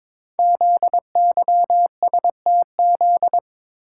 Morse code, heard as a keyed tone, about 22 wpm, 700 Hz. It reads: ZYSTZ
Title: Code Groups - 5 Letters - 22wpm